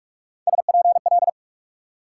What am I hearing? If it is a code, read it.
Morse code, heard as a keyed tone, about 63 wpm, spaced as for 54 wpm, 700 Hz. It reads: SPL
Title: Speed-Racing Multi-letter Phonemes - 45wpm